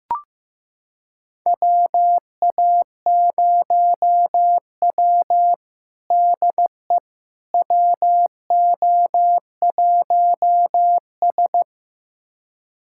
Morse code, heard as a keyed tone, about 15 wpm, 700 Hz. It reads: WA0W DE WO1S
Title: USA Callsign DE Callsign - 4 Characters - 15wpm